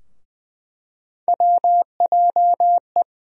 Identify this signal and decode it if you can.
Morse code, heard as a keyed tone, about 20 wpm, 700 Hz. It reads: WJE
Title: USA Call Sign Postfixes - 3 Letters 20wpm